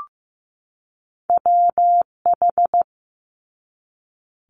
Morse code, heard as a keyed tone, about 15 wpm, 700 Hz. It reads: WH
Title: Multi-letter Phonemes 15wpm